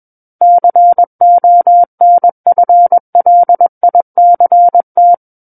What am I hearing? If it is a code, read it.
Morse code, heard as a keyed tone, about 21 wpm, 700 Hz. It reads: CONFLICT
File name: Book One / 12-Chapter 12 - 21 WPM